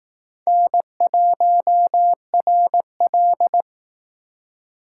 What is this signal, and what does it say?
Morse code, heard as a keyed tone, about 18 wpm, 700 Hz. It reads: N1RL
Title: Speed Racing - USA 1x2 Call Signs 15wpm